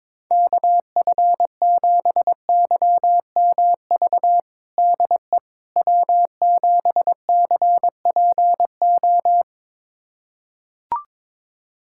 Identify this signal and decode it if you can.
Morse code, heard as a keyed tone, about 22 wpm, 700 Hz. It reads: KF7YMV DE W7CPO
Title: USA Callsign DE Callsign 22wpm